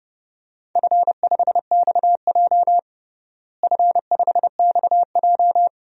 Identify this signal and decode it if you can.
Morse code, heard as a keyed tone, about 30 wpm, 700 Hz. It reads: F5XJ F5XJ
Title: DXCC-100 Call Signs - 4 Characters - Repeated 2x - 30wpm